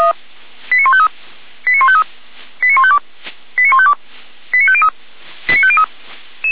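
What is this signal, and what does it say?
Five-tone selective call (ZVEI), heard as quick runs of five tones, tones of about 70 ms. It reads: 89153 89153 89152 89152 89503 89503
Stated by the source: ZVEI five-tone call received by SDR, Feuerwehr Orlamünde.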